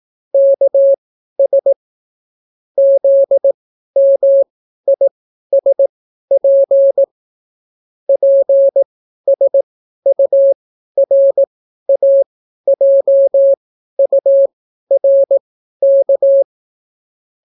Morse code, heard as a keyed tone, about 18 wpm, spaced as for 12 wpm, 550 Hz. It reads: KS ZMISP PSURAJURK